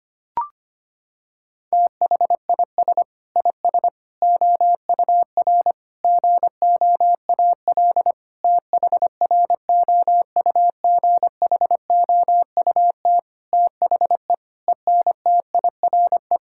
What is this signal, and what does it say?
Morse code, heard as a keyed tone, about 25 wpm, 700 Hz. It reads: THIS IS OUR GOAL THROUGHOUT THE ENTIRE
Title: Sentences from Top 1000 Words - Review Part 5 - 25wpm